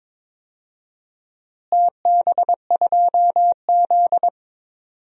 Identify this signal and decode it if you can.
Morse code, heard as a keyed tone, about 22 wpm, 700 Hz. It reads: TB2Z
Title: DXCC-100 Call Signs - 4 Characters - Repeated 2x - 22wpm